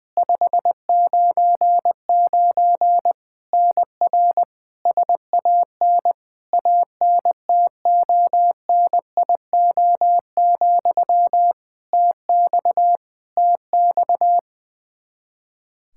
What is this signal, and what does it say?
Morse code, heard as a keyed tone, about 20 wpm, 700 Hz. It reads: 599 NR SAN ANTONIO, TX TX
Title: K1USN SST QSO Practice 20wpm